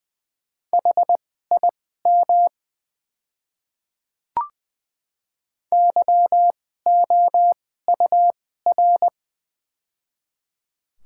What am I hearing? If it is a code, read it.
Morse code, heard as a keyed tone, about 20 wpm, spaced as for 14 wpm, 690 Hz. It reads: HIM YOUR
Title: 2x Character Spacing - Top 100 Words - 20wpm